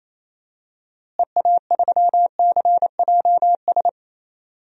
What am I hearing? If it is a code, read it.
Morse code, heard as a keyed tone, about 28 wpm, 700 Hz. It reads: EA3CJS
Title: DXCC-100 Call Signs - 6 Characters - Repeated 2x - 28wpm